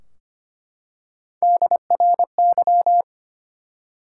Morse code, heard as a keyed tone, about 25 wpm, 700 Hz. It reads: DRY
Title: Three Letter Words - 25wpm